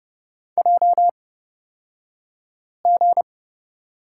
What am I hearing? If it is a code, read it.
Morse code, heard as a keyed tone, about 30 wpm, spaced as for 5 wpm, 700 Hz. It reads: JG
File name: Lesson 37d - Letter x - Two Characters 30wpm